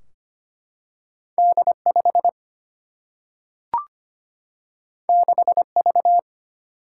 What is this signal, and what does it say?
Morse code, heard as a keyed tone, about 25 wpm, 700 Hz. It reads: D 5 6 V
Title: Single-Letter Single-Number 25wpm